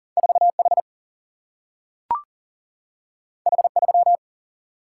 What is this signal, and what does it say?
Morse code, heard as a keyed tone, about 40 wpm, 700 Hz. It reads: 4 H H 3
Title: Single-Letter Single-Number 40wpm